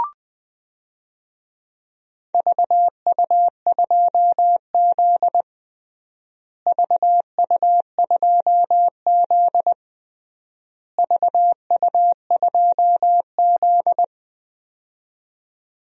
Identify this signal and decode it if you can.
Morse code, heard as a keyed tone, about 20 wpm, 700 Hz. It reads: VU2Z VU2Z VU2Z
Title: DXCC-100 Call Signs - 4 Characters - Repeated 3x - 20wpm